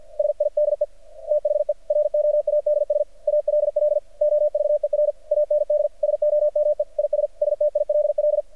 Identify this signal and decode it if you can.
Morse code, heard as a keyed tone, about 49 wpm, 600 Hz. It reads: SIDE THE ROADS ALL OVER AND SOME IS STILL
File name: archived / CRSnet